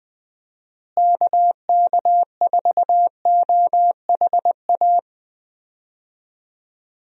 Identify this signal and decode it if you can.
Morse code, heard as a keyed tone, about 20 wpm, 700 Hz. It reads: KK4OHA
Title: USA Call Signs - 6 Characters - Repeated 3x - 20wpm